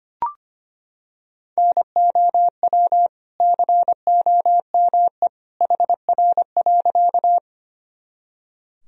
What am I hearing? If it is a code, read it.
Morse code, heard as a keyed tone, about 25 wpm, 700 Hz. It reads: NOW COME HR.
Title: Sentences from Top 200 Words and Abbreviations 25wpm